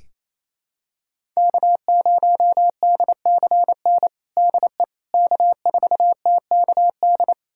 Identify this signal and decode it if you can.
Morse code, heard as a keyed tone, about 28 wpm, 700 Hz. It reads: K0DCN DE K4TKD
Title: USA Callsign DE Callsign - 5 Characters - 28wpm